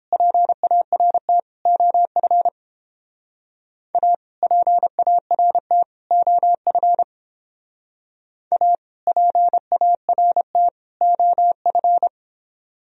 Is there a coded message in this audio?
Yes: Morse code from 0.1 to 12.1 s, heard as a keyed tone, about 30 wpm, 700 Hz. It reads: PART OF A PART OF A PART OF